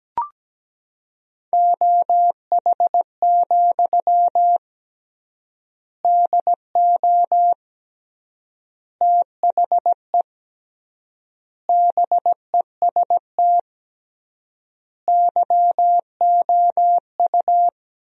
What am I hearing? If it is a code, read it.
Morse code, heard as a keyed tone, about 17 wpm, 700 Hz. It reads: OH, DO THE BEST YOU
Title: 3x Word-Spacing - Sentences from Top 500 Words - 17wpm